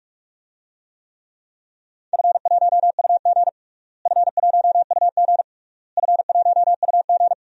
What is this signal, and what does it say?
Morse code, heard as a keyed tone, about 46 wpm, 690 Hz. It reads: F1UG F1UG F1UG
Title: DXCC-100 Call Signs - 4 Characters - Repeated 3x - 45wpm